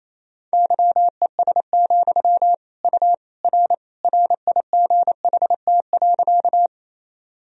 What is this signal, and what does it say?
Morse code, heard as a keyed tone, about 28 wpm, 700 Hz. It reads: YES, U R RIGHT.
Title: Sentences from Top 500 Words and Abbreviations 28wpm